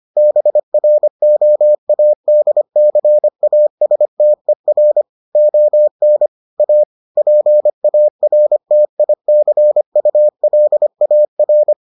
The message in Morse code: BROADCASTER ON A PARTICULAR

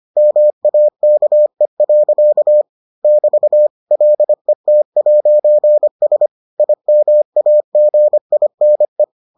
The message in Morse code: MAKE. <BT> LET'S IMAGINE

